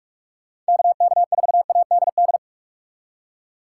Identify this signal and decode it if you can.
Morse code, heard as a keyed tone, about 44 wpm, 700 Hz. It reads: KK4ADD
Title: Speed Racing - USA 2x3 Call Signs 45wpm